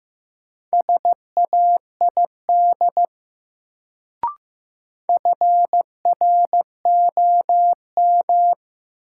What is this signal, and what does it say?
Morse code, heard as a keyed tone, about 15 wpm, 700 Hz. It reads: SAID FROM
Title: Sets of 2 Words - New Word Drill - 15wpm